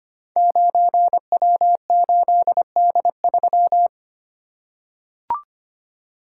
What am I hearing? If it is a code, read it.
Morse code, heard as a keyed tone, about 25 wpm, 700 Hz. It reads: 9W8D3
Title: Code Groups - 5 Letter:Numbers - 25wpm